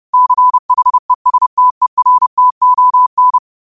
Morse code, heard as a keyed tone, about 30 wpm, 1000 Hz. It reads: CHESTERTON
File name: qotdfiles / 30wpm